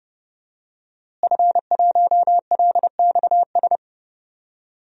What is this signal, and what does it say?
Morse code, heard as a keyed tone, about 30 wpm, 700 Hz. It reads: F1LXS
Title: DXCC-100 Call Signs - 5 Characters - Repeated 3x - 30wpm